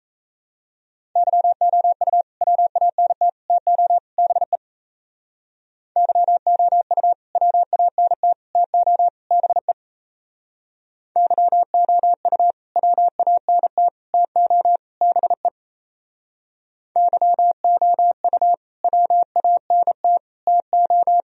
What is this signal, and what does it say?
Morse code, heard as a keyed tone, about 37 wpm, 700 Hz. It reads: YOU WANT TO BE YOU WANT TO BE YOU WANT TO BE YOU WANT TO